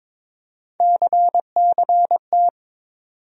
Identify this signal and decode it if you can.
Morse code, heard as a keyed tone, about 22 wpm, 700 Hz. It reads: CCT
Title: USA Call Sign Postfixes - 3 Letters 22wpm